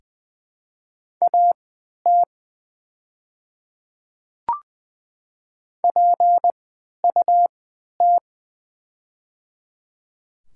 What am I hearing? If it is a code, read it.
Morse code, heard as a keyed tone, about 20 wpm, spaced as for 11 wpm, 700 Hz. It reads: AT PUT
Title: 3x Character Spacing - Top 300 Words - 20wpm